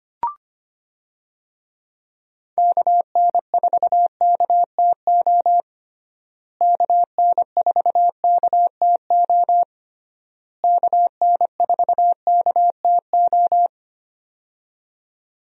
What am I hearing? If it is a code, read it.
Morse code, heard as a keyed tone, about 25 wpm, 700 Hz. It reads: KN4KTO KN4KTO KN4KTO
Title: USA Call Signs - 6 Characters - Repeated 3x - 25wpm